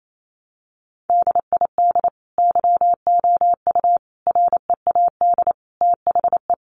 Morse code, heard as a keyed tone, about 28 wpm, 700 Hz. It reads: DID YOU READ THE
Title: Sentences from Top 1200 Words 28wpm